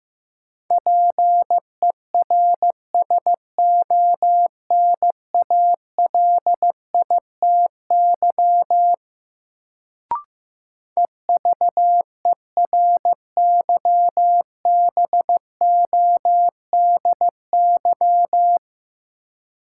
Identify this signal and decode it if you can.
Morse code, heard as a keyed tone, about 15 wpm, 700 Hz. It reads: PERSONALITY EVERYBODY
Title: Sentences from Top 1400 Words - New Word Drill - 15wpm